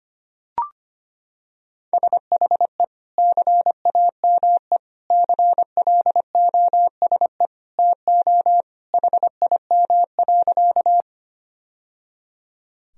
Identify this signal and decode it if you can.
Morse code, heard as a keyed tone, about 25 wpm, 700 Hz. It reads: SHE CAME CLOSE TO HIM.
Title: Sentences from Top 1000 Words - Review Part 1 - 25wpm